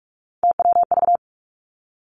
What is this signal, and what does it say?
Morse code, heard as a keyed tone, about 46 wpm, 700 Hz. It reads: TWV
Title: Three Challenging Letter-Number Combinations 45wpm